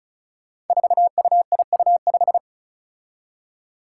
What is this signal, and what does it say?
Morse code, heard as a keyed tone, about 34 wpm, 700 Hz. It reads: 4UIU5
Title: Five Challenging Letter-Number Combinations 35wpm